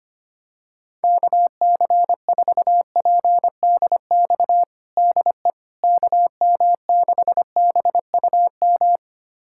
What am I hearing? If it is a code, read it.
Morse code, heard as a keyed tone, about 25 wpm, 700 Hz. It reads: KC4PDX DE KM6BUM
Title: USA Callsign DE Callsign 25wpm